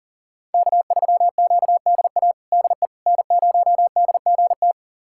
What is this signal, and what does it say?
Morse code, heard as a keyed tone, about 40 wpm, 700 Hz. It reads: K3QDA DE N0DGT
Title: USA Callsign DE Callsign - 5 Characters - 40wpm